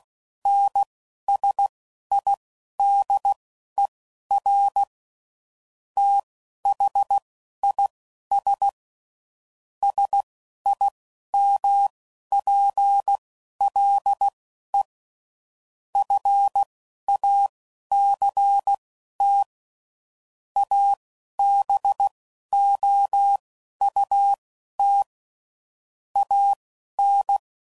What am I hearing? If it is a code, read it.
Morse code, heard as a keyed tone, about 16 wpm, spaced as for 11 wpm, 800 Hz. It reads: NSIDER THIS SIMPLE FACT ABOUT AN
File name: CW Code Groups @10WPM/Technical Subjects